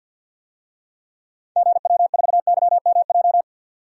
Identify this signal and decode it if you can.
Morse code, heard as a keyed tone, about 52 wpm, 700 Hz. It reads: GW4YMJ
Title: DXCC-100 Call Signs - 6 Characters - 50wpm